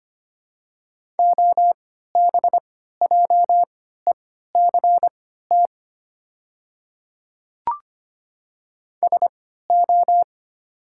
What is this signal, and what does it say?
Morse code, heard as a keyed tone, about 25 wpm, spaced as for 14 wpm, 690 Hz. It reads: OBJECT SO
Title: 3x Character Spacing - Top 500 Words - 25wpm